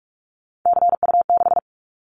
Morse code, heard as a keyed tone, about 44 wpm, 700 Hz. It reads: CU6
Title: Callsign Prefixes for Top 100 DXCC - 45wpm